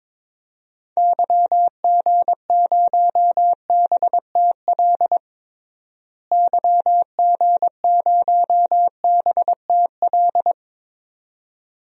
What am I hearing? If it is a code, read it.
Morse code, heard as a keyed tone, about 22 wpm, 700 Hz. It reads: YG0BTL YG0BTL
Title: DXCC-100 Call Signs - 6 Characters - Repeated 2x - 22wpm